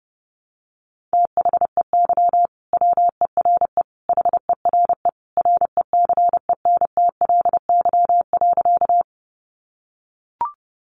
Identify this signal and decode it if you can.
Morse code, heard as a keyed tone, about 30 wpm, 700 Hz. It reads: THEY WERE HERE RECENTLY.
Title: Sentences from Top 1200 Words 30wpm